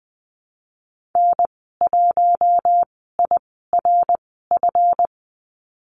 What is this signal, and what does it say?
Morse code, heard as a keyed tone, about 20 wpm, spaced as for 15 wpm, 700 Hz. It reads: N1IRF
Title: 173 Lesson 28g - Number 9 - Call Signs 20wpm